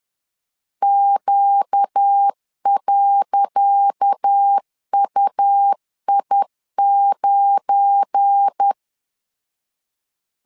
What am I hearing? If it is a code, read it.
Morse code, heard as a keyed tone, about 10 wpm, 800 Hz. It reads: Q.UI9
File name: LEZIONI / 10 SIGNS / 10C DOT 10wpm 2940